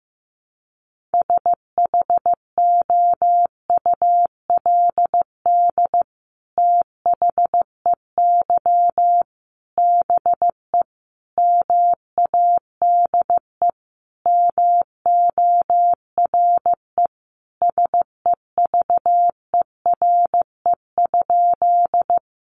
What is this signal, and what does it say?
Morse code, heard as a keyed tone, about 14 wpm, 700 Hz. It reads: SHOULD THEY BE MADE MORE SEVERE?
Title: Sentences from Top 1400 Words 15wpm